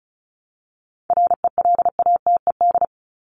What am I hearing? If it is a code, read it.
Morse code, heard as a keyed tone, about 34 wpm, 700 Hz. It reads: RELATED
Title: ED Suffix - Top 100 Words with Base Words 35wpm